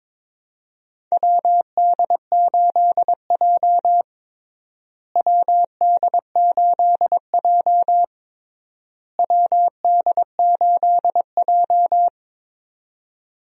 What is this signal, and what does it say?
Morse code, heard as a keyed tone, about 22 wpm, 700 Hz. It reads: WD8J WD8J WD8J
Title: USA Call Signs - 4 Characters - Repeated 3x - 22wpm